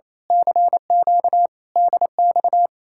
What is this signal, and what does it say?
Morse code, heard as a keyed tone, about 27 wpm, 700 Hz. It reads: CQ DX
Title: Speed-Racing Common QSO Phrases Encore - 28wpm